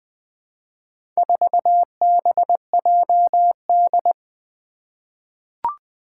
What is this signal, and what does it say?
Morse code, heard as a keyed tone, about 20 wpm, 700 Hz. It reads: 4BJD
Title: Four Challenging Letter-Number Combinations 20wpm